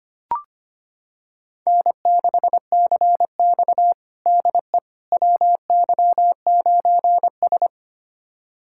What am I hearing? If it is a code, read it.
Morse code, heard as a keyed tone, about 25 wpm, 700 Hz. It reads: N6CX DE WY9S
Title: USA Callsign DE Callsign - 4 Characters - 25wpm